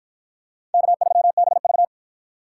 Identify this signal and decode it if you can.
Morse code, heard as a keyed tone, about 53 wpm, 700 Hz. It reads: K3BV